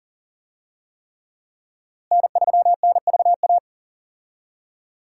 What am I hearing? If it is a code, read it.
Morse code, heard as a keyed tone, about 40 wpm, 700 Hz. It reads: N3NVA
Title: USA Call Signs - 5 Characters - Repeated 3x - 40wpm